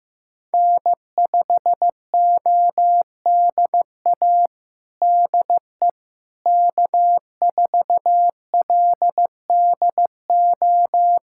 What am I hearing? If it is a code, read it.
Morse code, heard as a keyed tone, about 15 wpm, 700 Hz. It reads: N5ODA DE K4LDO
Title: USA Callsign DE Callsign - 5 Characters - 15wpm